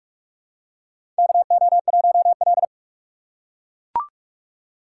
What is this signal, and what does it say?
Morse code, heard as a keyed tone, about 42 wpm, 700 Hz. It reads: KO1L
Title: USA Call Signs - 4 Characters - 2x Character Spacing - 45wpm